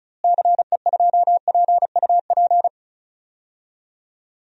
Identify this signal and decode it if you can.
Morse code, heard as a keyed tone, about 34 wpm, 700 Hz. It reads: CE2PUP
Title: DXCC-100 Call Signs - 6 Characters - Repeated 3x - 35wpm